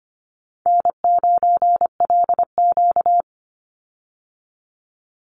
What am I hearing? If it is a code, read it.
Morse code, heard as a keyed tone, about 25 wpm, 700 Hz. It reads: N9LQ
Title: Call Signs Repeated 2x - 25wpm